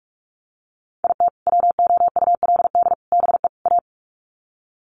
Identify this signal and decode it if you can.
Morse code, heard as a keyed tone, about 44 wpm, 700 Hz. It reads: IT WOULD BE A
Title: Sets of 4 Words 45wpm